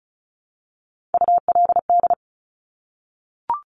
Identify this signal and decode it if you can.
Morse code, heard as a keyed tone, about 36 wpm, 700 Hz. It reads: ULD
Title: Top Three Letter Combinations in Words 35wpm